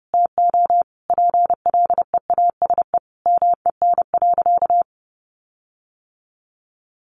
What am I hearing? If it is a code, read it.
Morse code, heard as a keyed tone, about 30 wpm, 700 Hz. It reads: TO PLEASE MEN.
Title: Sentences from Top 1500 Words 30wpm